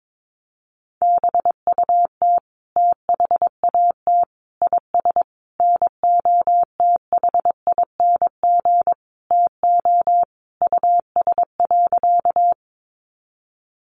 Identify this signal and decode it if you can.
Morse code, heard as a keyed tone, about 22 wpm, 700 Hz. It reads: BUT THAT IS NOTHING TO US.
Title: Sentences from Top 500 Words 22wpm